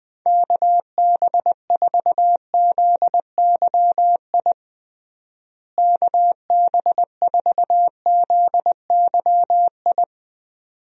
Morse code, heard as a keyed tone, about 20 wpm, 700 Hz. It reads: KB4ZYI KB4ZYI
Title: USA Call Signs - 6 Characters - Repeated 2x - 20wpm